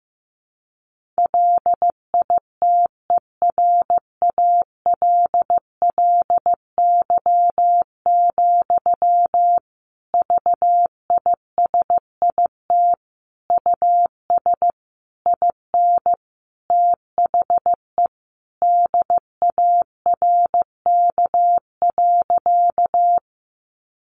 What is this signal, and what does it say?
Morse code, heard as a keyed tone, about 14 wpm, 700 Hz. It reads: LITERALLY, VISIT US IN THE DARK.
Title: Sentences from Top 1400 Words 15wpm